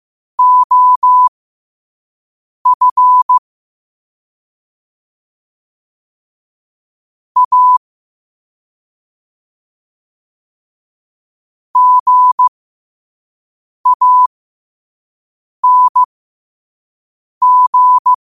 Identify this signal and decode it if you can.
Morse code, heard as a keyed tone, about 15 wpm, spaced as for 5 wpm, 1000 Hz. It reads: OF A GANG
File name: qotdfiles / 5wpm